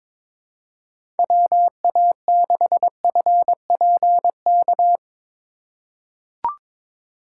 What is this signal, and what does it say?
Morse code, heard as a keyed tone, about 22 wpm, 700 Hz. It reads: WA6FPK
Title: USA Call Signs - 6 Characters - 2x Character Spacing - 22wpm